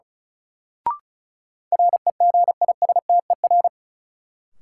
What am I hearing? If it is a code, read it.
Morse code, heard as a keyed tone, about 34 wpm, 700 Hz. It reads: REGISTER
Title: Sentences from Top 1200 Words - New Word Drill - 35wpm